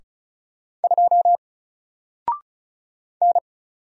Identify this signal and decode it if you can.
Morse code, heard as a keyed tone, about 36 wpm, 700 Hz. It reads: 2 N
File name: Single-Letter-Number-35wpm